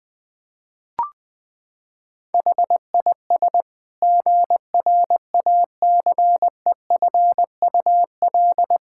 Morse code, heard as a keyed tone, about 20 wpm, 700 Hz. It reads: HIS GRACEFUL
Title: Pangram Challenge v2 - 20wpm